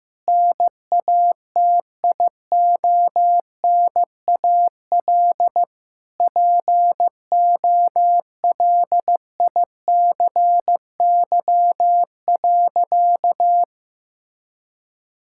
Morse code, heard as a keyed tone, about 15 wpm, 700 Hz. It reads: NATIONAL POLICY.
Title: Sentences from Top 1000 Words - Review Part 6 - 15wpm